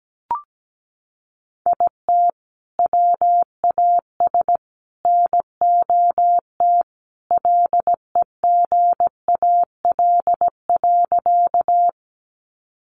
Morse code, heard as a keyed tone, about 17 wpm, 700 Hz. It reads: IT WAS NOT LEGAL.
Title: Sentences from Top 1200 Words 17wpm